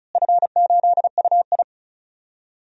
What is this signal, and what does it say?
Morse code, heard as a keyed tone, about 36 wpm, 690 Hz. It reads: F8UI
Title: DXCC-100 Call Signs - 4 Characters - Repeated 2x - 35wpm